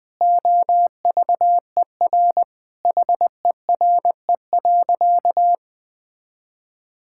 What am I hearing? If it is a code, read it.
Morse code, heard as a keyed tone, about 21 wpm, 700 Hz. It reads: OVER HERE.
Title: Sentences with 4-Word Phrases - 20wpm